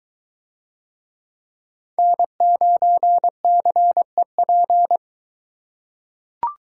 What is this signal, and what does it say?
Morse code, heard as a keyed tone, about 23 wpm, 700 Hz. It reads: N9CEP